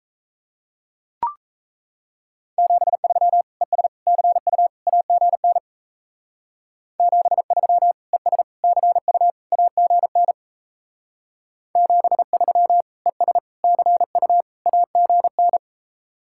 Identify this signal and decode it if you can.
Morse code, heard as a keyed tone, about 34 wpm, 700 Hz. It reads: 73 ES CU AGN 73 ES CU AGN 73 ES CU AGN
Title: Speed-Racing Common QSO Phrases - 28wpm